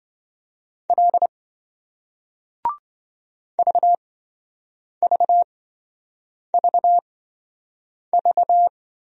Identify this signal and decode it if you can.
Morse code, heard as a keyed tone, about 28 wpm, spaced as for 7 wpm, 700 Hz. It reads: L VVVV